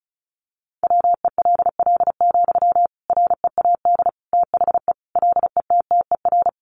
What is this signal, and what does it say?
Morse code, heard as a keyed tone, about 34 wpm, 700 Hz. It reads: WELL, READ THE LETTER
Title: Sentences from Top 300 Words 35wpm